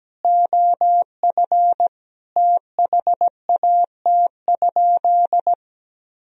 Morse code, heard as a keyed tone, about 17 wpm, 700 Hz. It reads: OF THAT?
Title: Sentences from Top 1000 Words - Review Part 2 - 17wpm